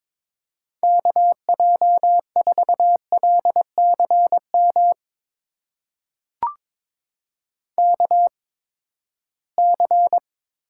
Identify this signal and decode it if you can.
Morse code, heard as a keyed tone, about 22 wpm, spaced as for 15 wpm, 700 Hz. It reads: KJ4LCM KC
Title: USA Call Signs - 6 Characters - 8x Character Spacing - 22wpm